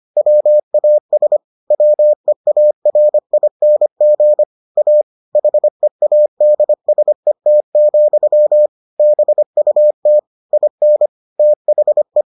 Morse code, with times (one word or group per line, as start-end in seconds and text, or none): 0.2-1.4 WAS
1.7-4.4 WEARING
4.8-5.0 A
5.3-8.7 HEADSET,
9.0-10.2 BUT
10.5-11.1 IN
11.4-12.2 THE